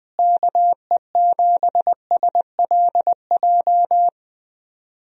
Morse code, heard as a keyed tone, about 21 wpm, 700 Hz. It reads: KE7SLJ
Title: USA Call Signs - 6 Characters - Repeated 3x - 20wpm